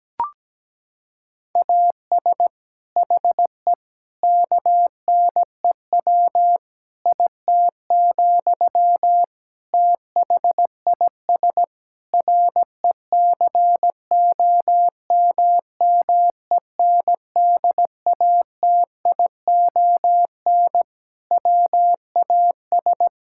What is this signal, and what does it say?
Morse code, heard as a keyed tone, about 17 wpm, 700 Hz. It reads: AS HE KNEW IT, THIS RECOMMENDATION WAS
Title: Sentences from Top 1600 Words 17wpm